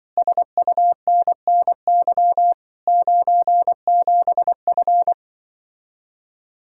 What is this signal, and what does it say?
Morse code, heard as a keyed tone, about 24 wpm, 700 Hz. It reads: SUNNY 97F
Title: Speed-Racing Common QSO Phrases - 20wpm